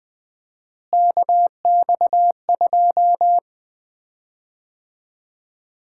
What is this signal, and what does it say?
Morse code, heard as a keyed tone, about 20 wpm, 700 Hz. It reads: KX2
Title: Common QSO Phrases - 4x Word-Spacing - 20wpm